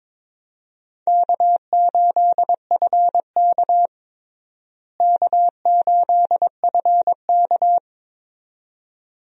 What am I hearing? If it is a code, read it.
Morse code, heard as a keyed tone, about 22 wpm, 700 Hz. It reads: K8FK K8FK
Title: USA Call Signs - 4 Characters - Repeated 2x - 22wpm